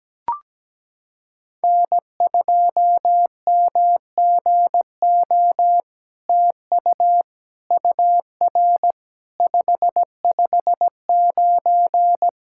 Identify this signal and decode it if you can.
Morse code, heard as a keyed tone, about 17 wpm, 700 Hz. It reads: N2MGO TU UR 559
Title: SOTA QSO Practice 17wpm